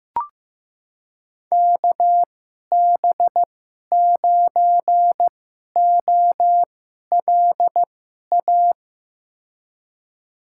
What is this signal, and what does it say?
Morse code, heard as a keyed tone, about 15 wpm, spaced as for 11 wpm, 700 Hz. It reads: KB9OLA
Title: USA Call Signs - 2x Character Spacing - 15wpm